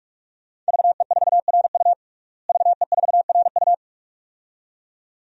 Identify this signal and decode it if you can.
Morse code, heard as a keyed tone, about 46 wpm, 700 Hz. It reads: VE4RU VE4RU
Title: DXCC-100 Call Signs - 5 Characters - Repeated 2x - 45wpm